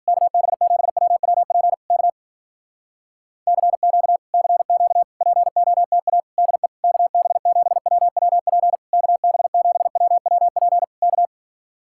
Morse code, heard as a keyed tone, about 53 wpm, 700 Hz. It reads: KB7WWP K CQ CQ POTA DE KB7WWP KB7WWP K